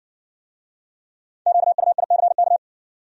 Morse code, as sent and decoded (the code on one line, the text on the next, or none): --.- ..- .. -.-. -.-
QUICK